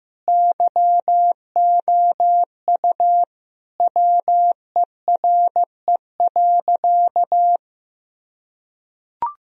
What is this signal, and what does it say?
Morse code, heard as a keyed tone, about 15 wpm, 700 Hz. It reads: YOU WERE.
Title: Sentences from Top 1000 Words - Review Part 4 - 15wpm